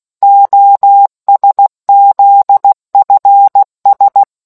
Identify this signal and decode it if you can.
Morse code, heard as a keyed tone, about 16 wpm, 800 Hz. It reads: OSZFS